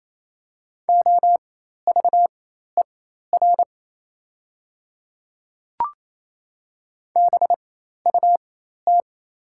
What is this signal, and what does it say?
Morse code, heard as a keyed tone, about 28 wpm, spaced as for 13 wpm, 700 Hz. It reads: OVER BUT
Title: 4x Character Spacing - Top 100 Words - 28wpm